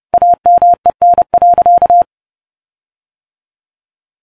Morse code, heard as a keyed tone, about 28 wpm, 700 Hz. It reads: AMEN.